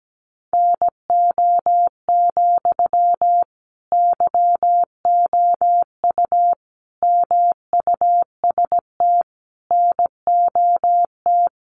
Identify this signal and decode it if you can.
Morse code, heard as a keyed tone, about 17 wpm, 700 Hz. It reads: NO, YOU MUST NOT
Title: Sentences from Top 1200 Words 17wpm